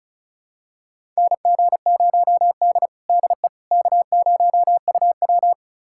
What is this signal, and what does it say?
Morse code, heard as a keyed tone, about 34 wpm, 700 Hz. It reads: NG0D DE K0UW
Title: USA Callsign DE Callsign - 4 Characters - 35wpm